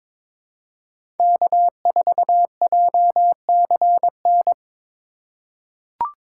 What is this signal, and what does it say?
Morse code, heard as a keyed tone, about 22 wpm, 700 Hz. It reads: K4JCN
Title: USA Call Signs - 5 Characters - 2x Character Spacing - 22wpm